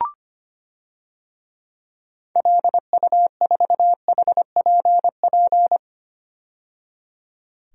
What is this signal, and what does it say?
Morse code, heard as a keyed tone, about 25 wpm, 700 Hz. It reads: LU4HPP
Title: DXCC-100 Call Signs - 6 Characters - 25wpm